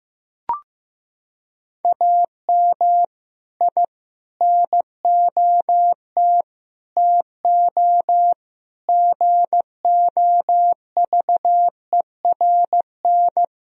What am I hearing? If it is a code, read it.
Morse code, heard as a keyed tone, about 16 wpm, 700 Hz. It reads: AM I NOT TO GOVERN
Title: Sentences from Top 1000 Words - Review Part 2 - 15wpm